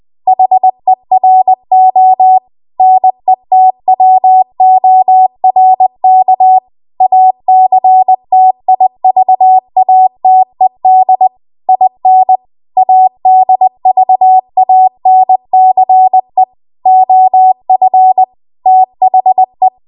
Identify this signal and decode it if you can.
Morse code, heard as a keyed tone, about 20 wpm, 750 Hz. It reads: HERO NETWORK ACTIVATED IN ADVANCE OF THE
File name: Archive / 20 WPM / 20WPM